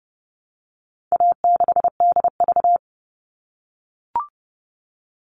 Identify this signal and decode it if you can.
Morse code, heard as a keyed tone, about 30 wpm, 700 Hz. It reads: A6DV
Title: All Cut Numbers - 4-Digit 30wpm